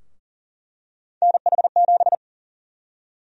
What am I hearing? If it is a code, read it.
Morse code, heard as a keyed tone, about 40 wpm, 700 Hz. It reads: NH7
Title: USA Call Sign Prefixes - Repeated 2x - 40wpm